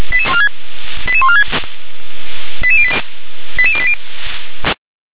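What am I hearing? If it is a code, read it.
Five-tone selective call (ZVEI), heard as quick runs of five tones, tones of about 70 ms. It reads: 88146 88080
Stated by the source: Received via SDR as a ZVEI five-tone call, address 88702.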